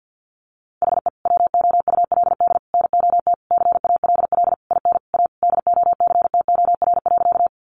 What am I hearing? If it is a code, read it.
Morse code, heard as a keyed tone, about 50 wpm, 700 Hz. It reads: HE WOULD NOT CALL IN A DOCTOR.
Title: Sentences from Top 1500 Words 50wpm